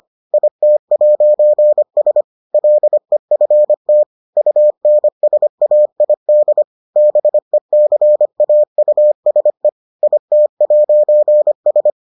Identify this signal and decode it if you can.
Morse code, heard as a keyed tone, about 25 wpm, 600 Hz. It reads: IT'S LEFT UNSAID BECAUSE IT'S